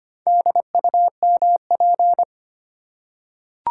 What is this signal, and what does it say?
Morse code, heard as a keyed tone, about 25 wpm, 700 Hz. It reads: DUMP
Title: Four Letter Words - 25wpm